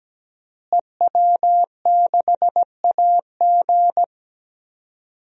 Morse code, heard as a keyed tone, about 17 wpm, 700 Hz. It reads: EW6AG
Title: DXCC-100 Call Signs - 5 Characters - Repeated 3x - 17wpm